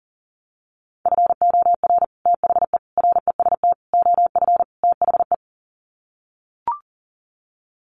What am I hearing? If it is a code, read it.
Morse code, heard as a keyed tone, about 40 wpm, 700 Hz. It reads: FOR THE REST OF THE
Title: Sets of 5 Words 40wpm